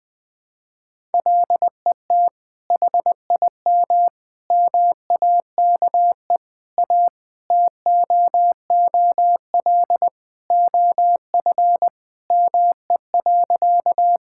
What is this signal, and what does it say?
Morse code, heard as a keyed tone, about 20 wpm, 700 Hz. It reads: LET HIM MAKE A TOOL OF ME.
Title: Sentences from Top 1000 Words - Review Part 6 - 20wpm